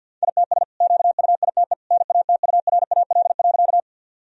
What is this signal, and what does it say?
Morse code, heard as a keyed tone, about 50 wpm, 700 Hz. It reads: ITS QUITE NATURAL.
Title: Sentences from Top 1000 Words - Review Part 4 - 50wpm